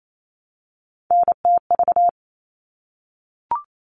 Morse code, heard as a keyed tone, about 28 wpm, 700 Hz. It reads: NTV
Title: All Cut Numbers - 3-Digit 28wpm